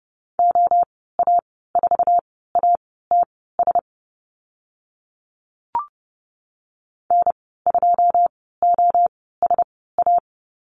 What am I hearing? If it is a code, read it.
Morse code, heard as a keyed tone, about 30 wpm, spaced as for 17 wpm, 700 Hz. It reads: OA4ATS N2OSA
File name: Lesson 15g - Number 5 - Call Signs 30wpm